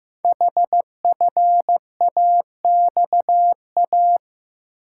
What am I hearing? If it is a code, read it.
Morse code, heard as a keyed tone, about 15 wpm, 700 Hz. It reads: HFAXA
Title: Code Groups - 5 Letters - 15wpm